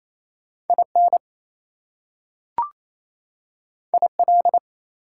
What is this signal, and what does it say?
Morse code, heard as a keyed tone, about 28 wpm, 700 Hz. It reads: IN IL
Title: US States and Canadian Provinces 28wpm